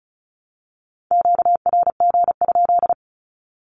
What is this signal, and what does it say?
Morse code, heard as a keyed tone, about 34 wpm, 700 Hz. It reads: QRG?
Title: Q-Codes 35wpm